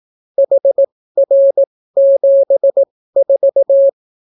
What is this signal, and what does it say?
Morse code, heard as a keyed tone, about 18 wpm, spaced as for 14 wpm, 550 Hz. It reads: HR74